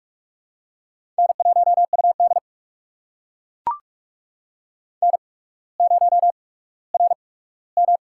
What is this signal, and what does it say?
Morse code, heard as a keyed tone, about 42 wpm, spaced as for 19 wpm, 700 Hz. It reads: N1UD N0RM
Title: USA Call Signs - 4 Characters - 8x Character Spacing - 45wpm